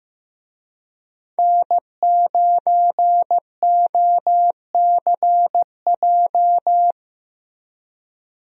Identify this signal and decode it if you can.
Morse code, heard as a keyed tone, about 15 wpm, 700 Hz. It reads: N9OCJ
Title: USA Call Signs - 5 Characters - Repeated 3x - 15wpm